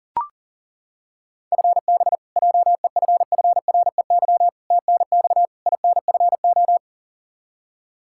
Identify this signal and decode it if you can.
Morse code, heard as a keyed tone, about 40 wpm, 700 Hz. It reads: FB JEFFREY TNX INFO
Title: K1USN SST QSO Practice 40wpm